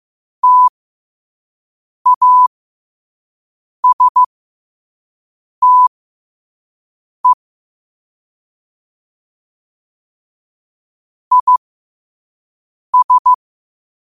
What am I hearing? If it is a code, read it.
Morse code, heard as a keyed tone, about 15 wpm, spaced as for 5 wpm, 1000 Hz. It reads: TASTE IS
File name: qotdfiles / 5wpm